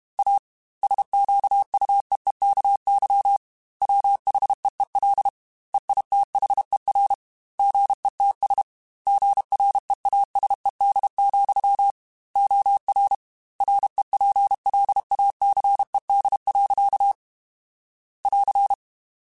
Morse code, heard as a keyed tone, about 31 wpm, 800 Hz. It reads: A SQUEEKY WHEEL EITHER GETS GREASED, OR REPLACED. <AR>
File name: Chuckles / Chuckles-mp3s / Squeaky @30